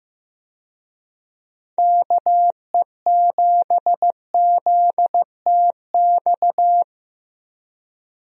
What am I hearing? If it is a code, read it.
Morse code, heard as a keyed tone, about 15 wpm, 700 Hz. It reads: KE7ZTX
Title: USA Call Signs - 6 Characters - Repeated 3x - 15wpm